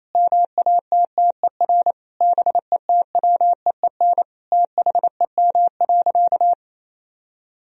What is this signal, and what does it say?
Morse code, heard as a keyed tone, about 28 wpm, 700 Hz. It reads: MATTER BETWEEN THEM.
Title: Sentences from Top 1000 Words - Review Part 4 - 28wpm